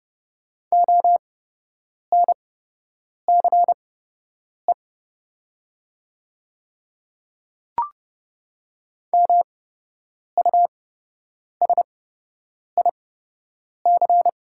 Morse code, heard as a keyed tone, about 30 wpm, spaced as for 8 wpm, 700 Hz. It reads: ONCE MUSIC
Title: 8x Character Spacing - Top 300 Words - 30wpm